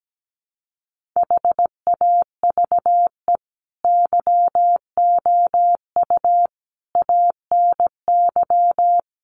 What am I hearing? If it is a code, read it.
Morse code, heard as a keyed tone, about 17 wpm, 700 Hz. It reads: HAVE YOU ANY
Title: Sentences from Top 1400 Words 17wpm